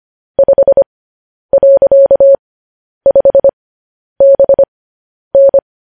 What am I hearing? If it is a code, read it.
Morse code, heard as a keyed tone, about 25 wpm, spaced as for 10 wpm, 550 Hz. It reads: 5.5BN
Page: news headlines in morse code at 10 wpm